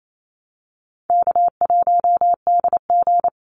Morse code, heard as a keyed tone, about 28 wpm, 700 Hz. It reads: K1DG
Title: Easy Call Signs - 28wpm